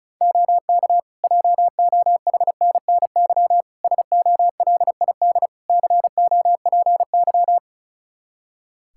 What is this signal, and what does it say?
Morse code, heard as a keyed tone, about 34 wpm, 700 Hz. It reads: OK JOHNNY SOLID COPY
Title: Practice QSO - Exchange 3 - More Info - 35wpm